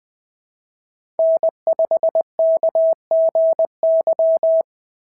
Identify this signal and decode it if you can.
Morse code, heard as a keyed tone, about 20 wpm, 650 Hz. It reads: N5KGY
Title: POTA QSO Practice - 20wpm